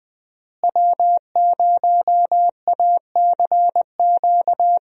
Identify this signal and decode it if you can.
Morse code, heard as a keyed tone, about 20 wpm, 700 Hz. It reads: W0ACQ
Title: USA Call Signs - 5 Characters - Repeated 3x - 20wpm